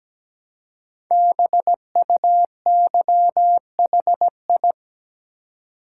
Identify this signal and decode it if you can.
Morse code, heard as a keyed tone, about 17 wpm, 700 Hz. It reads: BUYHI
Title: Code Groups - 5 Letters - 17wpm